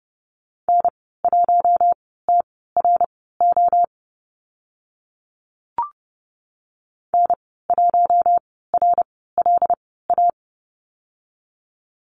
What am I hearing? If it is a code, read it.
Morse code, heard as a keyed tone, about 30 wpm, spaced as for 17 wpm, 700 Hz. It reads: N1TRO N1RLA
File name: Lesson 14g - Number 2 - Call Signs 30wpm